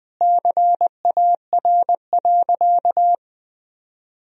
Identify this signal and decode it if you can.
Morse code, heard as a keyed tone, about 21 wpm, 700 Hz. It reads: CAR.
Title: Sentences from Top 1000 Words - Review Part 1 - 20wpm